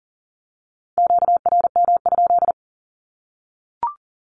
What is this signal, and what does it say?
Morse code, heard as a keyed tone, about 40 wpm, 700 Hz. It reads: QRM?
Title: Q-Codes Repeated 3x 40wpm